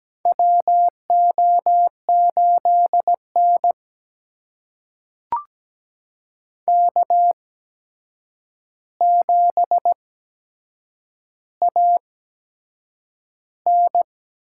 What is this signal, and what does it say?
Morse code, heard as a keyed tone, about 17 wpm, spaced as for 7 wpm, 700 Hz. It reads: WO8N K7AN
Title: USA Call Signs - 4 Characters - 8x Character Spacing - 17wpm